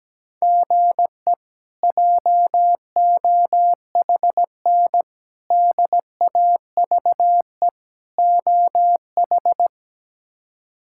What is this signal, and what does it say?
Morse code, heard as a keyed tone, about 17 wpm, 700 Hz. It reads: GE JOHN DAVE OH
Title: K1USN SST QSO Practice - 17wpm